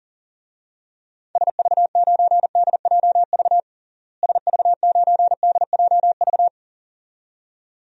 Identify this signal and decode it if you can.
Morse code, heard as a keyed tone, about 40 wpm, 700 Hz. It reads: SV9DJV SV9DJV
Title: DXCC-100 Call Signs - 6 Characters - Repeated 2x - 40wpm